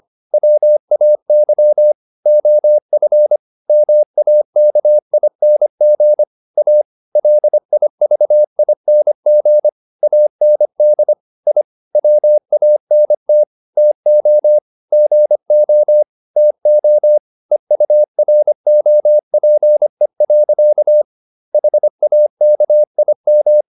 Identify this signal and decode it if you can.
Morse code, heard as a keyed tone, about 25 wpm, 600 Hz. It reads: WAY OF MAKING A LIVING AND I WANT TO GO TO EUROPE. HAKIM